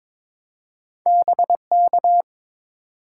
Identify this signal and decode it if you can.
Morse code, heard as a keyed tone, about 22 wpm, 700 Hz. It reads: BK
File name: Top 90 QSO Elements - 2x Character Spacing - 22wpm